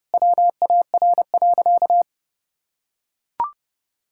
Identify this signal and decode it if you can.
Morse code, heard as a keyed tone, about 30 wpm, 700 Hz. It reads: WAR.
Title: Sentences from Top 1000 Words - Review Part 6 - 30wpm